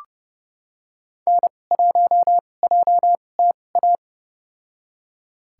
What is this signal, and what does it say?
Morse code, heard as a keyed tone, about 30 wpm, spaced as for 21 wpm, 690 Hz. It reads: N1JTA
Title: USA Call Signs - 5 Characters - 2x Character Spacing - 30wpm